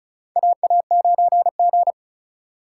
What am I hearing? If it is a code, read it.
Morse code, heard as a keyed tone, about 36 wpm, 700 Hz. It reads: AA9G
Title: USA Call Signs - 4 Characters - 4x Character Spacing - 35wpm